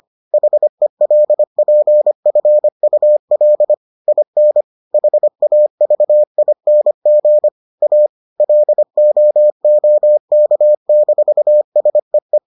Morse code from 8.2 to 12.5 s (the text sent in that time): LOOK-SEE